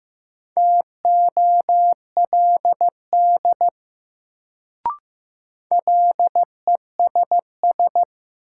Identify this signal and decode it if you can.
Morse code, heard as a keyed tone, about 15 wpm, 700 Hz. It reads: TOLD LESS
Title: Sentences from Top 400 Words - New Word Drill - 15wpm